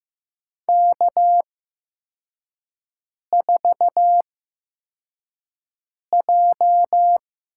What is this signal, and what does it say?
Morse code, heard as a keyed tone, about 15 wpm, spaced as for 4 wpm, 700 Hz. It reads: K4J
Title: USA Call Signs - 8x Character Spacing - 15wpm